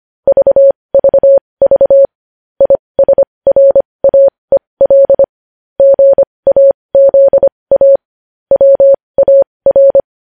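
Morse code, heard as a keyed tone, about 25 wpm, spaced as for 20 wpm, 550 Hz. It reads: VVV ISRAEL GAZA WAR
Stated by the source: Morse code news headlines sent at 20 wpm.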